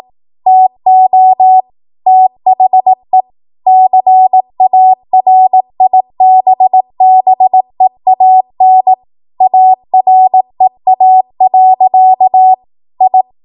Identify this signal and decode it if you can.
Morse code, heard as a keyed tone, about 18 wpm, 750 Hz. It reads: TO THE CARIBBEAN AREA. I
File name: Archive / 18 WPM / 18WPM